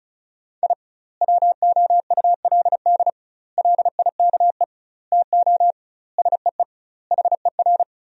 Morse code, heard as a keyed tone, about 36 wpm, 700 Hz. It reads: I WOULD LIKE TO SEE HER
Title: 2x Word-Spacing - Sentences from Top 200 Words - 35wpm